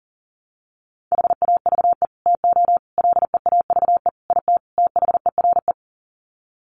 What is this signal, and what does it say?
Morse code, heard as a keyed tone, about 40 wpm, 700 Hz. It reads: HAVE TO LEAVE IT THERE
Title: Sets of 5 Words Encore2 40wpm